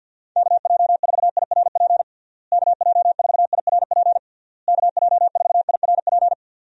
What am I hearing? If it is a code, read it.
Morse code, heard as a keyed tone, about 51 wpm, 700 Hz. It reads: KJ4IRP KJ4IRP KJ4IRP
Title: USA Call Signs - 6 Characters - Repeated 3x - 50wpm